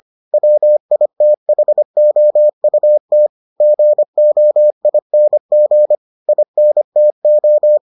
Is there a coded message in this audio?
Yes: Morse code from 0.3 to 7.8 s, heard as a keyed tone, about 25 wpm, 600 Hz. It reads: WITHOUT GOING INTO